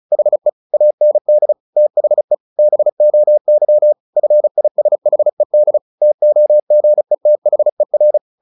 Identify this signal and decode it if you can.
Morse code, heard as a keyed tone, about 34 wpm, 600 Hz. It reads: HE AND THE BOY FISHED TOGETHER
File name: the old man and the sea / the old man and the sea 1 35wpm